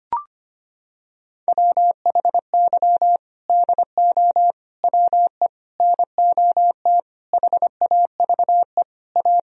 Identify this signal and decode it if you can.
Morse code, heard as a keyed tone, about 25 wpm, 700 Hz. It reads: WHY DO WE NOT HAVE A
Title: Sentences from Top 1000 Words - Review Part 3 - 25wpm